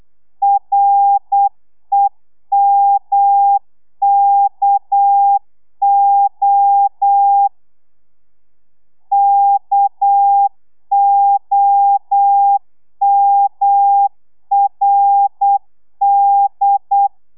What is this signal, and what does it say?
Morse code, heard as a keyed tone, about 8 wpm, 800 Hz. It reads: REMKO KOMRD
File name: telegrafia / i2tfj / Lez6 35